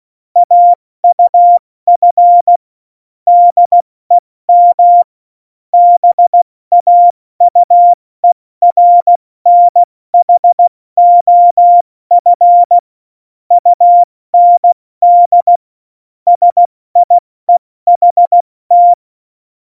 Morse code, heard as a keyed tone, about 16 wpm, 700 Hz. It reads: AUF DEM BAUERNHOF UND SIEHT
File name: Morsekurs / Witze / Witz 01. August 17, Tempo 70 F80